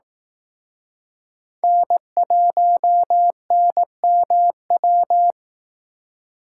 Morse code, heard as a keyed tone, about 18 wpm, 700 Hz. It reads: N1NMW
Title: Speed Racing - USA 1x3 Call Signs 15wpm